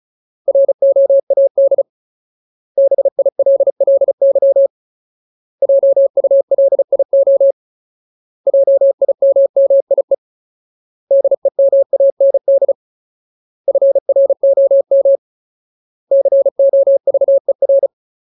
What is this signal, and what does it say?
Morse code, heard as a keyed tone, about 34 wpm, 550 Hz. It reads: ROAD BILLY JULIO JIMMIE DEMAND FROM COVER
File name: Ultimate Word List / 35WPM Ultimate